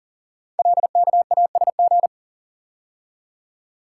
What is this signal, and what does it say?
Morse code, heard as a keyed tone, about 42 wpm, 690 Hz. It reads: LKASG
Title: Code Groups - 5 Letters - 40wpm